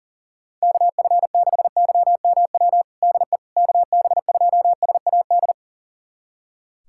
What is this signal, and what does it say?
Morse code, heard as a keyed tone, about 40 wpm, 700 Hz. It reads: KF6YMW DE KB2SAD
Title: USA Callsign DE Callsign 40wpm